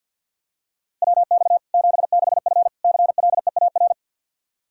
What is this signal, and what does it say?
Morse code, heard as a keyed tone, about 46 wpm, 700 Hz. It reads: WX 76F CLEAR
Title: K1USN SST QSO Practice 50wpm